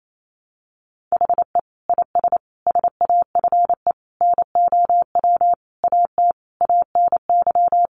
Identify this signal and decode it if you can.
Morse code, heard as a keyed tone, about 28 wpm, 700 Hz. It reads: HE IS SAFE NOW AT ANY
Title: Sentences from Top 900 Words 28wpm